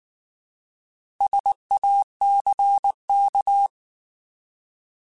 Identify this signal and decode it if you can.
Morse code, heard as a keyed tone, about 19 wpm, 800 Hz. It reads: SACK
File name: Alphabetical / Alphabetical mp3s / Ss @18